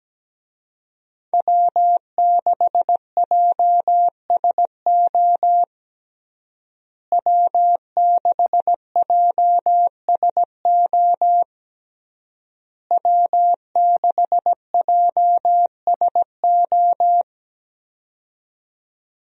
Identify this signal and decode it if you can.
Morse code, heard as a keyed tone, about 17 wpm, 700 Hz. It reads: W6JSO W6JSO W6JSO
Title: USA Call Signs - 5 Characters - Repeated 3x - 17wpm